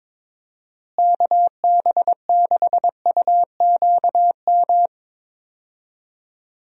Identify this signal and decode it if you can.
Morse code, heard as a keyed tone, about 21 wpm, 700 Hz. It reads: KB6UQM
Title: USA Call Signs - 6 Characters - Repeated 2x - 22wpm